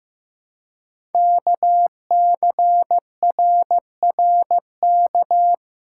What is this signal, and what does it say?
Morse code, heard as a keyed tone, about 15 wpm, 700 Hz. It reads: KCRRK
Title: Code Groups - 5 Letter:Numbers - 15wpm